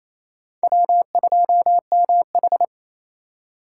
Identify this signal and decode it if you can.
Morse code, heard as a keyed tone, about 34 wpm, 700 Hz. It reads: W2MH
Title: Speed Racing - USA 1x2 Call Signs 28wpm